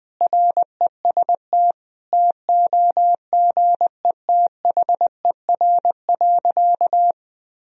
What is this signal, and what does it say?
Morse code, heard as a keyed tone, about 20 wpm, 700 Hz. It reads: REST TOGETHER.
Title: Sentences from Top 500 Words and Abbreviations 20wpm